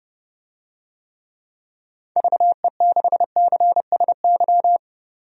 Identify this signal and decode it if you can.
Morse code, heard as a keyed tone, about 30 wpm, 700 Hz. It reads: VE6CSY
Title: DXCC-100 Call Signs - 6 Characters - 30wpm